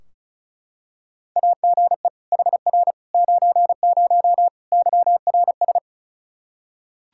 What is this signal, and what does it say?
Morse code, heard as a keyed tone, about 34 wpm, 700 Hz. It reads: AGE HR 90 YRS
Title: Practice QSO - Exchange 3 - More Info - 35wpm